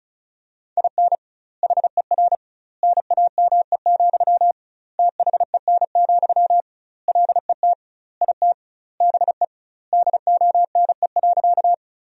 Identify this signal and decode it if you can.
Morse code, heard as a keyed tone, about 36 wpm, 700 Hz. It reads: IN HER NAME, THEN, LET IT BE DONE.
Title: 2x Word-Spacing - Sentences from Top 500 Words - 35wpm